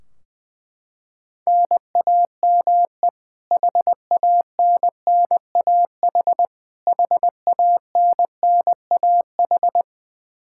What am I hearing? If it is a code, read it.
Morse code, heard as a keyed tone, about 20 wpm, 700 Hz. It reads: NAME HANNAH HANNAH
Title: Practice QSO - Exchange 1 - Beginning - 20wpm